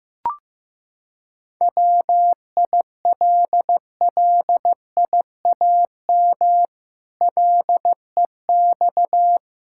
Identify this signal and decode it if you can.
Morse code, heard as a keyed tone, about 15 wpm, 700 Hz. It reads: WILLIAM LEX
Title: Pangram Challenge v2 - 15wpm